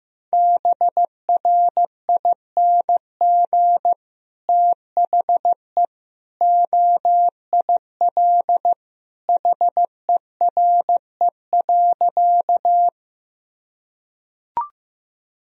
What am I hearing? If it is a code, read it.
Morse code, heard as a keyed tone, about 15 wpm, 700 Hz. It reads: BRING THE OIL HERE.
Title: Sentences from Top 1000 Words - Review Part 4 - 15wpm